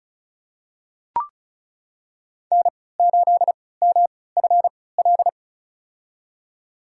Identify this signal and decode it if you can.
Morse code, heard as a keyed tone, about 34 wpm, spaced as for 20 wpm, 700 Hz. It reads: N8MFL
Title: USA Call Signs - 3x Character Spacing - 35wpm